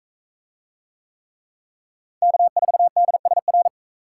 Morse code, heard as a keyed tone, about 42 wpm, 700 Hz. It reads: K4DSR